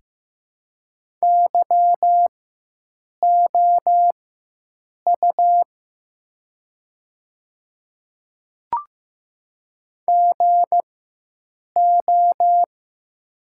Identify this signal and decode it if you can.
Morse code, heard as a keyed tone, about 15 wpm, spaced as for 7 wpm, 700 Hz. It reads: YOU GO
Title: 4x Character Spacing - Top 100 Words - 15wpm